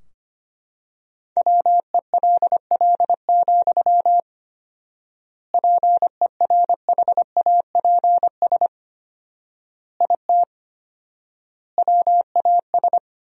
Morse code, heard as a keyed tone, about 25 wpm, 700 Hz. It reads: WELL, PERHAPS IT WAS
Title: 4x Word-Spacing - Sentences from Top 500 Words - 25wpm